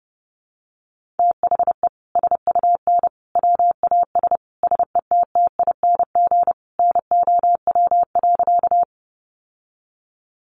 Morse code, heard as a keyed tone, about 28 wpm, 700 Hz. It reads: THE SUN WAS SETTING NOW.
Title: Sentences from Top 1300 Words 30wpm